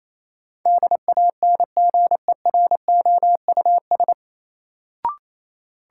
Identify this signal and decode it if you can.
Morse code, heard as a keyed tone, about 28 wpm, 700 Hz. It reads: DANGEROUS
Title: Sentences from Top 1200 Words - New Word Drill - 28wpm